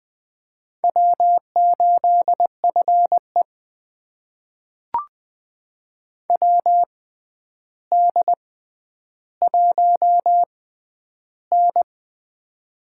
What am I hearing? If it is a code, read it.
Morse code, heard as a keyed tone, about 20 wpm, spaced as for 10 wpm, 700 Hz. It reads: W8FE WD1N
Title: USA Call Signs - 4 Characters - 6x Character Spacing - 20wpm